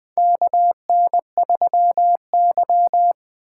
Morse code, heard as a keyed tone, about 20 wpm, 700 Hz. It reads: KN3Y
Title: 2023 SOTA Chaser Call Signs - 20wpm